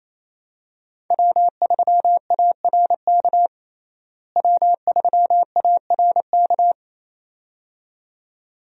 Morse code, heard as a keyed tone, about 28 wpm, 700 Hz. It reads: W3ARK W3ARK
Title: USA Call Signs - 5 Characters - Repeated 2x - 28wpm